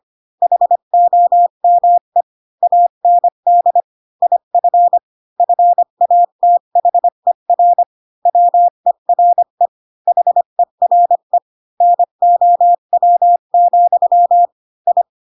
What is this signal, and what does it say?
Morse code, heard as a keyed tone, about 25 wpm, 700 Hz. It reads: HOME AND IF FATHER WERE HERE NOW, I
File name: Pinocchio / Pinocchio ch5 20 25 ik1hge